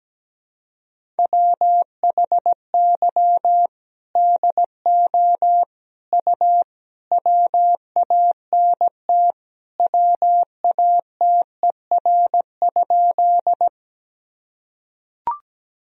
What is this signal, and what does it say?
Morse code, heard as a keyed tone, about 17 wpm, 700 Hz. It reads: WHY DO U WANT WATER?
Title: Sentences from Top 200 Words and Abbreviations 17wpm